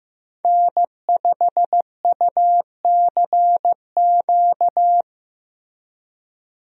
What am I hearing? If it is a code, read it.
Morse code, heard as a keyed tone, about 15 wpm, 700 Hz. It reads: N5UCQ
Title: USA Call Signs - 5 Characters - Repeated 3x - 15wpm